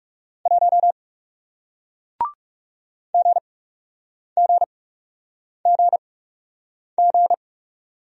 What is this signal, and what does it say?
Morse code, heard as a keyed tone, about 40 wpm, spaced as for 8 wpm, 700 Hz. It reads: 1 GGGG